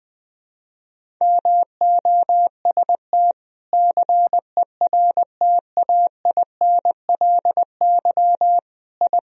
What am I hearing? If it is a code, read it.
Morse code, heard as a keyed tone, about 20 wpm, 700 Hz. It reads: MOST CERTAINLY I
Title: Sentences from Top 1000 Words - Review Part 1 - 20wpm